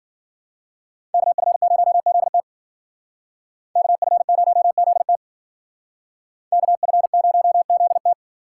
Morse code, heard as a keyed tone, about 48 wpm, 700 Hz. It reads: KF0ZT KF0ZT KF0ZT
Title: Speed Racing - USA 2x2 Call Signs 40wpm